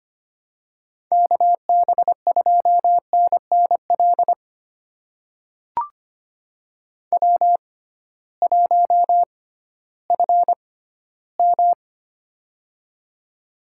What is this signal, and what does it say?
Morse code, heard as a keyed tone, about 25 wpm, spaced as for 15 wpm, 700 Hz. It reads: KB2NNL W1FM
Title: USA Call Signs - 6x Character Spacing - 25wpm